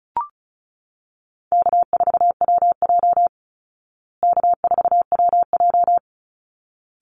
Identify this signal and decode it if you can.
Morse code, heard as a keyed tone, about 34 wpm, 700 Hz. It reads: K4WJ K4WJ
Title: Easy Call Signs Repeated 2x - 35wpm